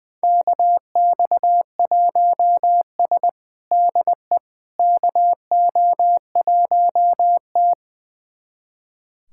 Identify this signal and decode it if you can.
Morse code, heard as a keyed tone, about 21 wpm, 700 Hz. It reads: KX1S DE KO1T
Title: USA Callsign DE Callsign - 4 Characters - 20wpm